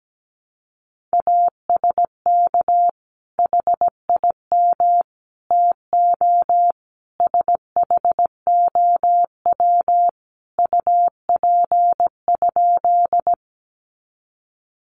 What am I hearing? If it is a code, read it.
Morse code, heard as a keyed tone, about 17 wpm, 700 Hz. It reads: ASK HIM TO SHOW UP?
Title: Sentences from Top 200 Words 17wpm